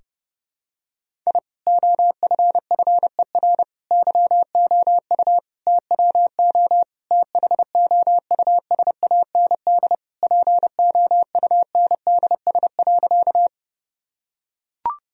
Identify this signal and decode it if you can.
Morse code, heard as a keyed tone, about 32 wpm, 700 Hz. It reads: I OFFER YOU TWO THOUSAND POUNDS.
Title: Sentences from Top 1000 Words - Review Part 4 - 30wpm